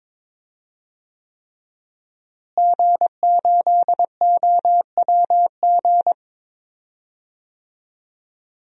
Morse code, heard as a keyed tone, about 22 wpm, 700 Hz. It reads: G8OWG
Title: DXCC-100 Call Signs - 5 Characters - Repeated 3x - 22wpm